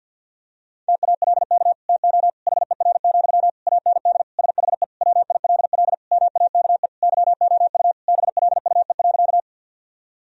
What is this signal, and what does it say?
Morse code, heard as a keyed tone, about 50 wpm, 700 Hz. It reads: TALK TO HER, AND SHE WILL MAKE YOU BLUE.
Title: Sentences from Top 1000 Words - Review Part 3 - 50wpm